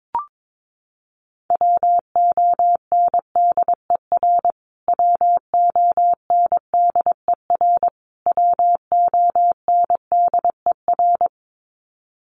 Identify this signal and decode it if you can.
Morse code, heard as a keyed tone, about 22 wpm, 700 Hz. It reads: WONDER WONDER WONDER
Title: Top 500 Words Repeated 22wpm